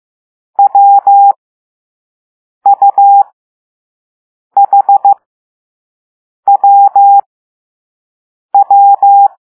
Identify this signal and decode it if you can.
Morse code, heard as a keyed tone, about 15 wpm, spaced as for 5 wpm, 800 Hz. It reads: WUHWW